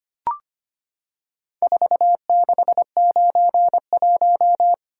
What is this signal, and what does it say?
Morse code, heard as a keyed tone, about 25 wpm, 700 Hz. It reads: 4691
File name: practice / 25wpm / Four Letter-Number Combinations - 25wpm